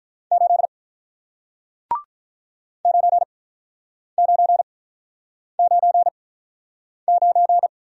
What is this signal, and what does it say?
Morse code, heard as a keyed tone, about 45 wpm, spaced as for 9 wpm, 700 Hz. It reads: 8 9999